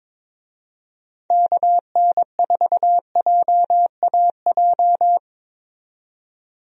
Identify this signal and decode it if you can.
Morse code, heard as a keyed tone, about 22 wpm, 700 Hz. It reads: KN4JAJ
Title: USA Call Signs - 6 Characters - 22wpm